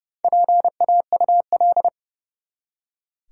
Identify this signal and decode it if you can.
Morse code, heard as a keyed tone, about 30 wpm, 700 Hz. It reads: PAUL
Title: USA Common Names - 30wpm